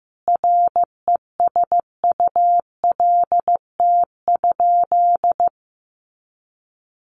Morse code, heard as a keyed tone, about 15 wpm, 700 Hz. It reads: RESULT?
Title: Sentences from Top 700 Words 15wpm